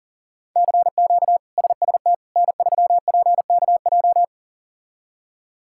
Morse code, heard as a keyed tone, about 37 wpm, 700 Hz. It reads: CQ SST N3PKJ
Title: K1USN SST QSO Practice - 40wpm